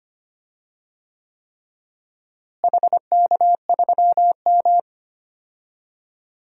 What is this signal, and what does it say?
Morse code, heard as a keyed tone, about 25 wpm, 700 Hz. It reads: HK3M
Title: DXCC-100 Call Signs - 4 Characters - Repeated 3x - 25wpm